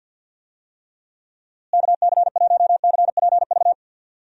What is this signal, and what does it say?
Morse code, heard as a keyed tone, about 52 wpm, 700 Hz. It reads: KC1CPV
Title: USA Call Signs - 6 Characters - Repeated 3x - 50wpm